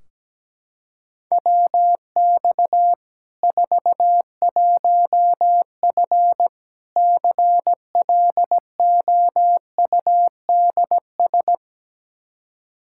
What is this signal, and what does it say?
Morse code, heard as a keyed tone, about 17 wpm, 700 Hz. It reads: WX 41F CLOUDS
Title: Practice QSO - Exchange 2 - Info - 17wpm